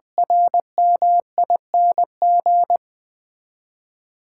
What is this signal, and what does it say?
Morse code, heard as a keyed tone, about 21 wpm, 700 Hz. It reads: RMING
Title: 8 Letter Words - First 3 Letters Given - 20wpm